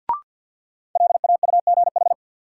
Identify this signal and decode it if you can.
Morse code, heard as a keyed tone, about 50 wpm, 700 Hz. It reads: LAUGH
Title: Sentences from Top 500 Words - New Word Drill - 50wpm